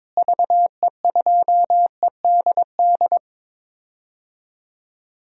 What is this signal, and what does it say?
Morse code, heard as a keyed tone, about 22 wpm, 700 Hz. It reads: VE2EDD
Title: DXCC-100 Call Signs - 6 Characters - Repeated 2x - 22wpm